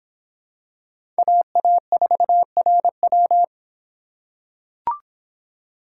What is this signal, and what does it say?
Morse code, heard as a keyed tone, about 23 wpm, 700 Hz. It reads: AA4RW